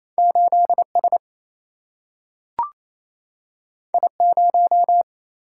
Morse code, heard as a keyed tone, about 28 wpm, 700 Hz. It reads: 8S I0
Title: Single-Letter Single-Number 28wpm